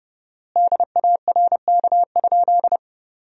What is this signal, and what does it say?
Morse code, heard as a keyed tone, about 30 wpm, 700 Hz. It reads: DARK?
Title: 4x Word-Spacing - Sentences from Top 500 Words - 30wpm